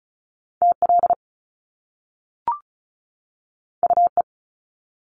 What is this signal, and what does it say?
Morse code, heard as a keyed tone, about 35 wpm, 700 Hz. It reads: TL UE
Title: Top Two Letter Combinations in Words 35wpm